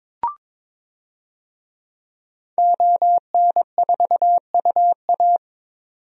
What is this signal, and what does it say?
Morse code, heard as a keyed tone, about 22 wpm, 700 Hz. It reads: ON4UA
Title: DXCC-100 Call Signs - 5 Characters - 22wpm